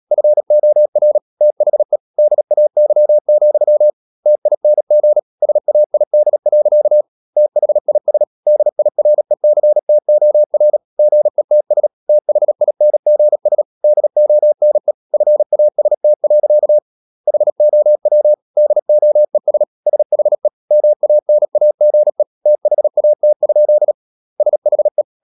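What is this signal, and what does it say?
Morse code, heard as a keyed tone, about 37 wpm, 600 Hz. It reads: FOR THE DAY, TING SAID. THIS DIRECTOR GETS THINGS DONE FAST. HOW DOES SHE MANAGE THAT? SHE
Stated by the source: Morse practice at 36 wpm, text of Ken Follett's novel Never, from CW med Gnister.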